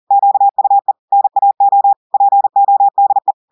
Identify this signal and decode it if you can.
Morse code, heard as a keyed tone, about 40 wpm, 800 Hz. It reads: QUE NAO PODE
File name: audios / portugais / portugais texte en clair 1 vitesse 40 mots minutes